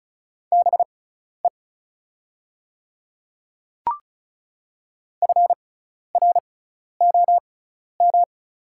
Morse code, heard as a keyed tone, about 36 wpm, spaced as for 12 wpm, 700 Hz. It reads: BE FROM
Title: 6x Character Spacing - Top 200 Words - 35wpm